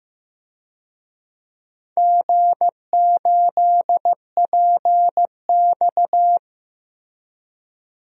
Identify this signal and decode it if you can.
Morse code, heard as a keyed tone, about 15 wpm, 700 Hz. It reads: G8PX
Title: DXCC-100 Call Signs - 4 Characters - Repeated 3x - 15wpm